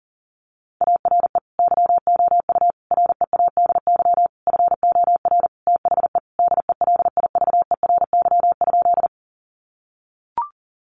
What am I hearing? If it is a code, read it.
Morse code, heard as a keyed tone, about 40 wpm, 700 Hz. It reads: ARE YOU READY FOR THE DELIVERY?
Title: Sentences from Top 1500 Words 40wpm